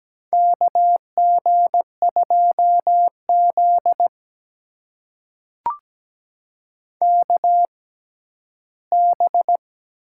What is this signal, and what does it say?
Morse code, heard as a keyed tone, about 17 wpm, spaced as for 11 wpm, 700 Hz. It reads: KG2Z KB
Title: USA Call Signs - 4 Characters - 6x Character Spacing - 17wpm